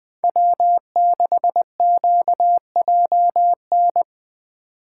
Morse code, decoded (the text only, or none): W6QJN